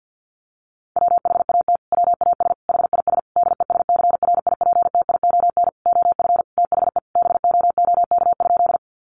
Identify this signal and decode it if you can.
Morse code, heard as a keyed tone, about 50 wpm, 700 Hz. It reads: WHAT WAS HIS DESCRIPTION OF THE BOOK?
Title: Sentences from Top 1300 Words 50wpm